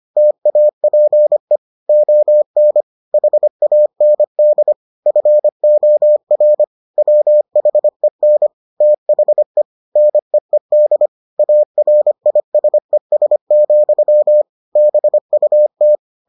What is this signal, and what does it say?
Morse code, heard as a keyed tone, about 25 wpm, 600 Hz. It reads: TAPE ON HAND FOR WHEN THE NEED ARISES, BUT